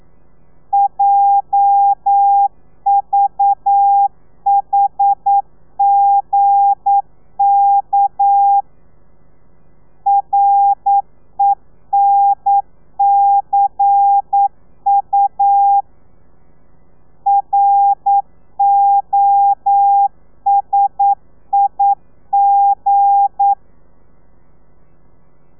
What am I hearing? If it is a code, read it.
Morse code, heard as a keyed tone, about 9 wpm, 800 Hz. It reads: JVHGK RENCU ROSIG
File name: telegrafia / i2tfj / Lez8 40